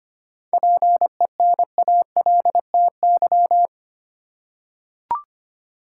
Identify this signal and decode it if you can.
Morse code, heard as a keyed tone, about 25 wpm, 700 Hz. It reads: PENALTY
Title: Seven Letter Words - 25wpm